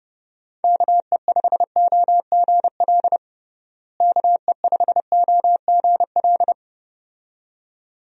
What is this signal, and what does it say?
Morse code, heard as a keyed tone, about 30 wpm, 700 Hz. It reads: KE5OGL KE5OGL
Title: USA Call Signs - 6 Characters - Repeated 2x - 30wpm